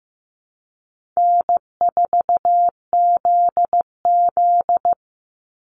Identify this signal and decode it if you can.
Morse code, heard as a keyed tone, about 15 wpm, 700 Hz. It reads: N4ZZ
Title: Call Signs Repeated 3x - 15wpm